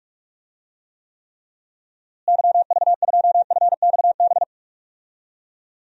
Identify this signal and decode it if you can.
Morse code, heard as a keyed tone, about 47 wpm, 700 Hz. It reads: YV2FXB
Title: DXCC-100 Call Signs - 6 Characters - Repeated 3x - 45wpm